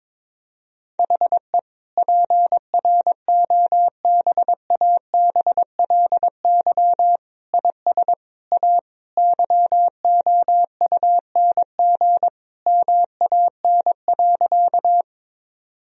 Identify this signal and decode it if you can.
Morse code, heard as a keyed tone, about 22 wpm, 700 Hz. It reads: HE PROBABLY IS A YOUNG MAN.
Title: Sentences from Top 1000 Words - Review Part 4 - 22wpm